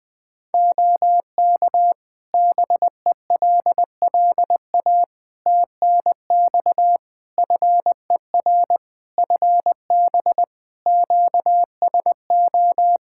Morse code, heard as a keyed tone, about 20 wpm, 700 Hz. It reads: OK BELLA TNX FER FB QSO
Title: K1USN SST QSO Practice 20wpm